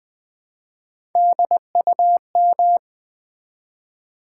Morse code, heard as a keyed tone, about 20 wpm, 700 Hz. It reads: DUM
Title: Build A Word - 1 Letter At A Time - 4-Letter Words - 20wpm